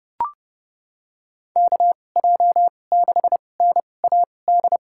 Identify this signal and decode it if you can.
Morse code, heard as a keyed tone, about 30 wpm, spaced as for 22 wpm, 700 Hz. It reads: KJ6NAD
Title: USA Call Signs - 6 Characters - 2x Character Spacing - 30wpm